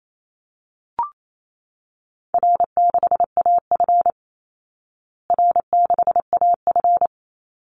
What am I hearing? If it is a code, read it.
Morse code, heard as a keyed tone, about 28 wpm, 700 Hz. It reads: R6AF R6AF
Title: Easy Call Signs Repeated 2x - 28wpm